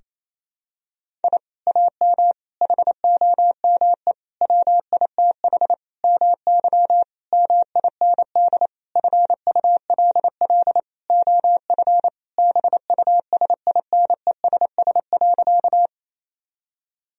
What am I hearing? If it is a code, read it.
Morse code, heard as a keyed tone, about 28 wpm, 700 Hz. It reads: I AM HOME WITH MY MIND FULL OF BUSINESS.
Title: Sentences from Top 1000 Words - Review Part 3 - 28wpm